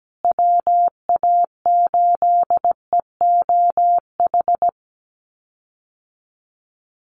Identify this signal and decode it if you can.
Morse code, heard as a keyed tone, about 17 wpm, 700 Hz. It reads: WA8EOH
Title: USA Call Signs Repeated 2x 17wpm